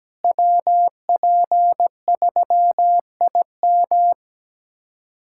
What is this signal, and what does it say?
Morse code, heard as a keyed tone, about 17 wpm, 700 Hz. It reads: WP3IM
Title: USA Callsign DE Callsign - 5 Characters - 17wpm